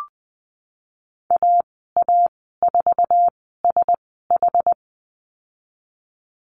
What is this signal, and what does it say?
Morse code, heard as a keyed tone, about 20 wpm, spaced as for 15 wpm, 700 Hz. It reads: AA4SH
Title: 154 Lesson 25g - Letter p - Call Signs 20wpm